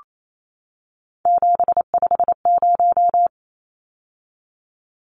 Morse code, heard as a keyed tone, about 28 wpm, 700 Hz. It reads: 750
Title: Numbers - 3-Digit 28wpm